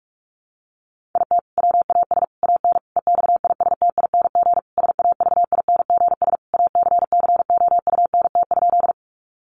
Morse code, heard as a keyed tone, about 44 wpm, 700 Hz. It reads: IT WAS AN EXISTING SAVINGS ACCOUNT?
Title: Sentences from Top 1500 Words 45wpm